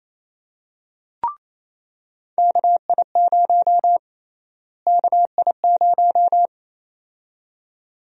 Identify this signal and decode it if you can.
Morse code, heard as a keyed tone, about 27 wpm, 700 Hz. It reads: KI0 KI0
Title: USA Call Sign Prefixes - Repeated 2x - 28wpm